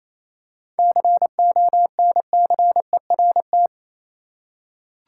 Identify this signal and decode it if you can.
Morse code, heard as a keyed tone, about 28 wpm, 700 Hz. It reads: CONCERT
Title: Seven Letter Words - 28wpm